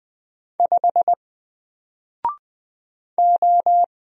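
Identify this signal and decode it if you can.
Morse code, heard as a keyed tone, about 20 wpm, 700 Hz. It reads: 5 O
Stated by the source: Single letter and number Morse drill at 20 wpm, from N4LCD.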